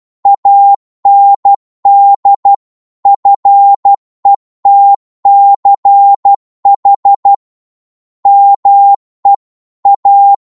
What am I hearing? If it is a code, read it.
Morse code, heard as a keyed tone, about 12 wpm, 800 Hz. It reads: AND FETCH ME A